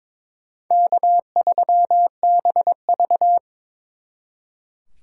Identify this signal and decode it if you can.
Morse code, heard as a keyed tone, about 25 wpm, 700 Hz. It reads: K3BV
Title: Speed Racing - USA 1x2 Call Signs 22wpm